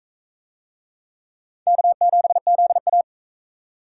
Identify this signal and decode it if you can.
Morse code, heard as a keyed tone, about 38 wpm, 700 Hz. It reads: K7ZA